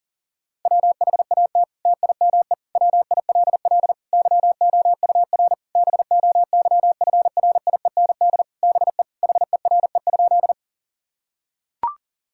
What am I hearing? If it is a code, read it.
Morse code, heard as a keyed tone, about 40 wpm, 700 Hz. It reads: WHAT TIME WILL YOUR BOYFRIEND BE HERE?
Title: Sentences from Top 1000 Words - Review Part 5 - 40wpm